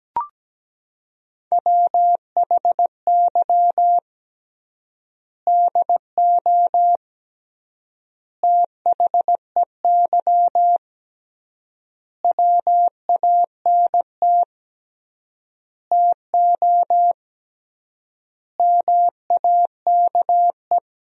Morse code, heard as a keyed tone, about 17 wpm, 700 Hz. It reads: WHY DO THEY WANT TO MAKE
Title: 3x Word-Spacing - Sentences from Top 400 Words - 17wpm